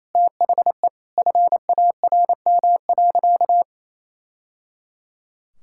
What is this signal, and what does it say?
Morse code, heard as a keyed tone, about 28 wpm, 700 Hz. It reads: THE FARM.
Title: Sentences from Top 1000 Words - Review Part 6 - 28wpm